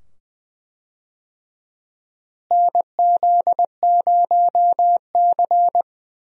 Morse code, heard as a keyed tone, about 20 wpm, 700 Hz. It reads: NZ0C
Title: USA Call Signs - 4 Characters - Repeated 2x - 20wpm